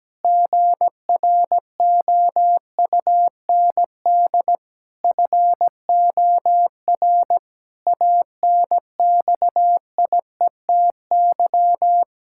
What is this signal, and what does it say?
Morse code, heard as a keyed tone, about 17 wpm, 700 Hz. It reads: GROUND FOR ANXIETY
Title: Sentences from Top 1600 Words 17wpm